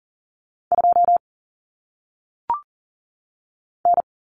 Morse code, heard as a keyed tone, about 40 wpm, 700 Hz. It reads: 2 N
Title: Common Cut Numbers - 1-Digit 40wpm